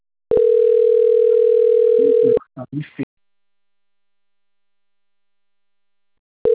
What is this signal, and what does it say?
Ringback tone (North America), 440 + 480 Hz, with speech.
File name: rg-1600-unknown- - - .71247